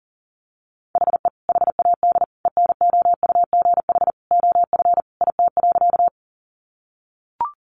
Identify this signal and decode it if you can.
Morse code, heard as a keyed tone, about 40 wpm, 700 Hz. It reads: HE HAD ENOUGH OF IT.
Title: Sentences from Top 400 Words 40wpm